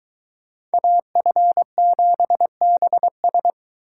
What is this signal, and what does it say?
Morse code, heard as a keyed tone, about 22 wpm, 700 Hz. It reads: AF7BS